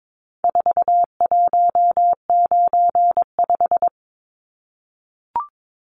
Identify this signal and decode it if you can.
Morse code, heard as a keyed tone, about 22 wpm, 700 Hz. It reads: 4195
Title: Numbers - 4-Digit 22wpm